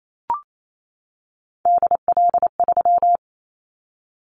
Morse code, heard as a keyed tone, about 28 wpm, 700 Hz. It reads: DL3
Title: Callsign Prefixes for Top 25 DXCC - 28wpm